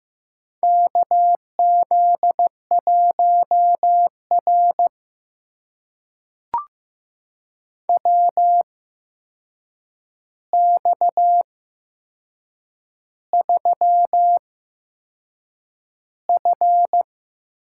Morse code, heard as a keyed tone, about 15 wpm, spaced as for 6 wpm, 700 Hz. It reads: KZ1R WX3F
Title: USA Call Signs - 4 Characters - 8x Character Spacing - 15wpm